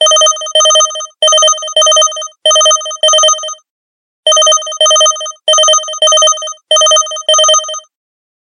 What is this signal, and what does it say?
There is no code.